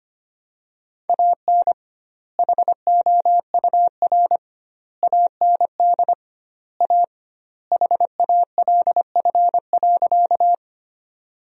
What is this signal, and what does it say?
Morse code, heard as a keyed tone, about 25 wpm, 700 Hz. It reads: AN HOUR AND A HALF.
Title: 2x Word-Spacing - Sentences from Top 400 Words - 25wpm